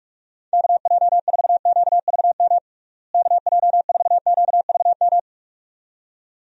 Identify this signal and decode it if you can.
Morse code, heard as a keyed tone, about 46 wpm, 700 Hz. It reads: KJ4QVM KJ4QVM
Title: USA Call Signs - 6 Characters - Repeated 2x - 45wpm